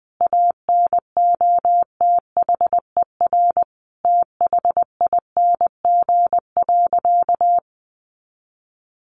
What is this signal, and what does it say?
Morse code, heard as a keyed tone, about 20 wpm, 700 Hz. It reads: ANOTHER THING.
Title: Sentences from Top 700 Words 20wpm